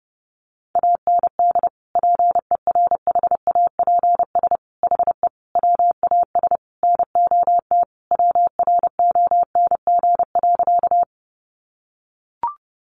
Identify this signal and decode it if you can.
Morse code, heard as a keyed tone, about 30 wpm, 700 Hz. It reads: AND PERHAPS HE WAS NOT WRONG.
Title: Sentences from Top 900 Words 30wpm